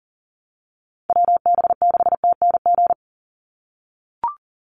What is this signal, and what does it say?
Morse code, heard as a keyed tone, about 40 wpm, 700 Hz. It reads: WB6TNG
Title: USA Call Signs 40wpm